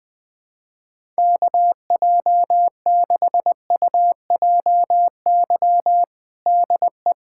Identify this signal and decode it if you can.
Morse code, heard as a keyed tone, about 20 wpm, 700 Hz. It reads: KJ6UJY DE
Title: USA Callsign DE Callsign - 6 Characters - 20wpm